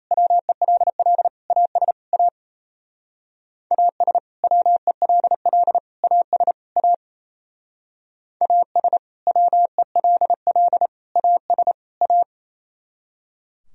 Morse code, heard as a keyed tone, about 35 wpm, 700 Hz. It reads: WELL AS A AS WELL AS A AS WELL AS A
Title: Speed-Racing - Sets of 4 Words 28wpm